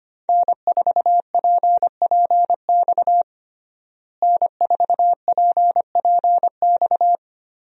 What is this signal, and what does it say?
Morse code, heard as a keyed tone, about 25 wpm, 690 Hz. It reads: N4PPX N4PPX
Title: USA Call Signs - 5 Characters - Repeated 2x - 25wpm